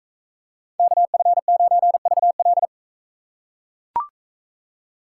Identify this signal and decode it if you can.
Morse code, heard as a keyed tone, about 40 wpm, 700 Hz. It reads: KF9VL